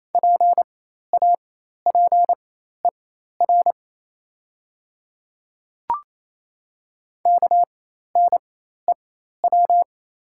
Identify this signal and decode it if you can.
Morse code, heard as a keyed tone, about 28 wpm, spaced as for 13 wpm, 700 Hz. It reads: PAPER KNEW
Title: 4x Character Spacing - Top 500 Words - 28wpm